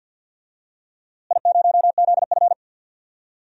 Morse code, heard as a keyed tone, about 46 wpm, 700 Hz. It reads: I0ZF
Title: DXCC-100 Call Signs - 4 Characters - Repeated 2x - 50wpm